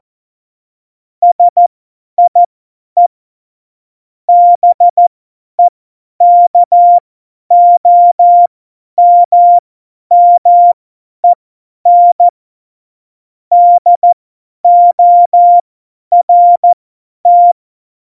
Morse code, heard as a keyed tone, about 14 wpm, spaced as for 10 wpm, 700 Hz. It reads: SIE BEKOMMEN DORT
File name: Morsekurs / Witze / Witz 06. Oktober 2020, 50 F70